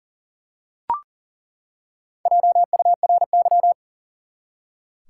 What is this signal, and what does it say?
Morse code, heard as a keyed tone, about 40 wpm, 700 Hz. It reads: JURY
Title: Four Letter Words - 40wpm